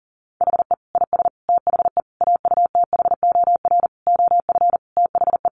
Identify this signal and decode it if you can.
Morse code, heard as a keyed tone, about 40 wpm, 700 Hz. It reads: HE IS THE AUTHOR OF THE
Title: Sentences from Top 1200 Words 40wpm